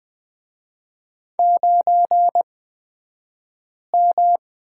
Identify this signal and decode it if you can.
Morse code, heard as a keyed tone, about 20 wpm, spaced as for 5 wpm, 700 Hz. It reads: 9M
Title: 242 Lesson 39d - Letter z - Two Characters 20wpm